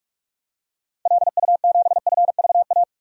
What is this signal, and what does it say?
Morse code, heard as a keyed tone, about 41 wpm, 700 Hz. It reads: LU7FVA
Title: DXCC-100 Call Signs - 6 Characters - 45wpm